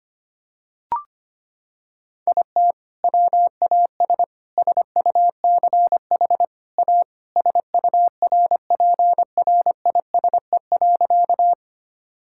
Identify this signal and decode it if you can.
Morse code, heard as a keyed tone, about 25 wpm, 700 Hz. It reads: IT WAS SUCH A SURPRISE.
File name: practice / 25wpm / Sentences with 2-Word Phrases - 25wpm